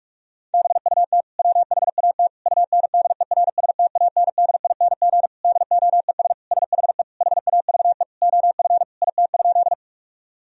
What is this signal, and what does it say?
Morse code, heard as a keyed tone, about 46 wpm, 700 Hz. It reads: BUT WHAT UNDERSTANDING DOES SHE HAVE OF IT?
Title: Sentences from Top 1000 Words - Review Part 3 - 45wpm